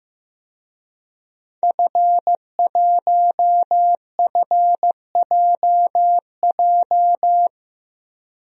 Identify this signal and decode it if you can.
Morse code, heard as a keyed tone, about 15 wpm, 700 Hz. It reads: F1FJJ
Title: DXCC-100 Call Signs - 5 Characters - 15wpm